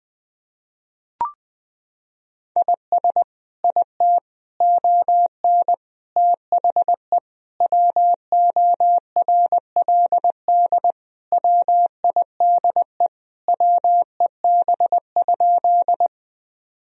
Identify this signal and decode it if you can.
Morse code, heard as a keyed tone, about 20 wpm, 700 Hz. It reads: IS IT ON THE WORLD WIDE WEB?
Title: Sentences from Top 1000 Words - Review Part 4 - 20wpm